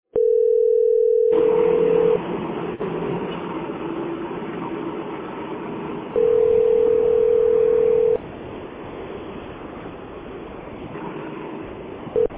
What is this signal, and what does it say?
Ringback tone (North America), 440 + 480 Hz.